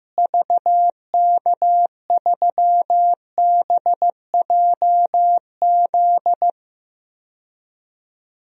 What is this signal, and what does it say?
Morse code, heard as a keyed tone, about 15 wpm, 700 Hz. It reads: VK3BJZ
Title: DXCC-100 Call Signs - 6 Characters - Repeated 3x - 15wpm